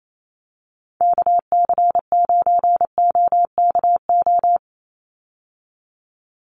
Morse code, heard as a keyed tone, about 28 wpm, 700 Hz. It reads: KC9OKO
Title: USA Call Signs 28wpm